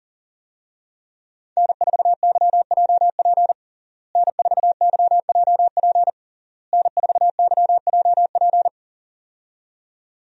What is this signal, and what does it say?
Morse code, heard as a keyed tone, about 40 wpm, 700 Hz. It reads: N4YJP N4YJP N4YJP
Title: USA Call Signs - 5 Characters - Repeated 3x - 40wpm